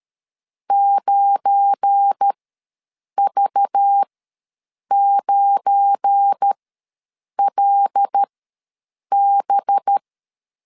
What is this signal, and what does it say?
Morse code, heard as a keyed tone, about 13 wpm, spaced as for 7 wpm, 800 Hz. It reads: 9V9LB
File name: LEZIONI / 7 GF9Z / 7 GF9 ws 12wpm 3344